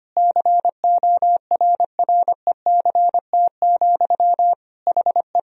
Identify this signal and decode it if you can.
Morse code, heard as a keyed tone, about 25 wpm, 700 Hz. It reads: CORRECT, HE
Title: Sentences from Top 1000 Words - Review Part 2 - 25wpm